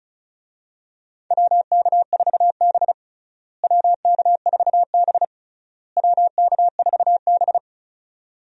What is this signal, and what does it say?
Morse code, heard as a keyed tone, about 34 wpm, 700 Hz. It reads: WK4B WK4B WK4B
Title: USA Call Signs - 4 Characters - Repeated 3x - 35wpm